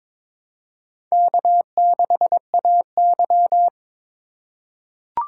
Morse code, heard as a keyed tone, about 22 wpm, 700 Hz. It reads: K6AY
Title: USA Call Signs - 4 Characters - 4x Character Spacing - 22wpm